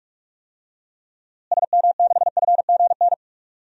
Morse code, heard as a keyed tone, about 41 wpm, 700 Hz. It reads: SM6FGN
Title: DXCC-100 Call Signs - 6 Characters - Repeated 3x - 45wpm